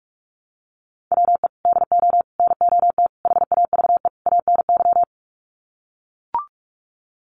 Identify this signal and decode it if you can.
Morse code, heard as a keyed tone, about 44 wpm, 700 Hz. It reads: WE DO NOT HAVE ANY
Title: Sets of 5 Words 45wpm